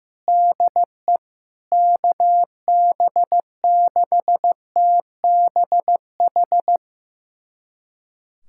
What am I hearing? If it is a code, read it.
Morse code, heard as a keyed tone, about 15 wpm, 700 Hz. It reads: DE KB6TBH
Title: USA Callsign DE Callsign - 6 Characters - 15wpm